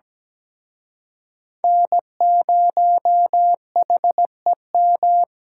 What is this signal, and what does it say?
Morse code, heard as a keyed tone, about 20 wpm, 700 Hz. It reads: N0HEM
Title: Speed Racing - USA 1x3 Call Signs 17wpm